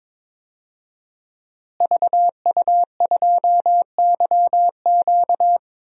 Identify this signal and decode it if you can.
Morse code, heard as a keyed tone, about 22 wpm, 700 Hz. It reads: VU2YQ
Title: DXCC-100 Call Signs - 5 Characters - Repeated 3x - 22wpm